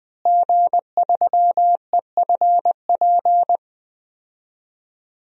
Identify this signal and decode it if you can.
Morse code, heard as a keyed tone, about 20 wpm, 700 Hz. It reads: G3EFP
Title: DXCC-100 Call Signs - 5 Characters - Repeated 3x - 20wpm